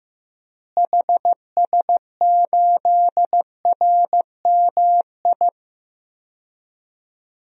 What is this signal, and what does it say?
Morse code, heard as a keyed tone, about 16 wpm, 700 Hz. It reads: HS8RMI
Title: DXCC-100 Call Signs - 6 Characters - Repeated 3x - 15wpm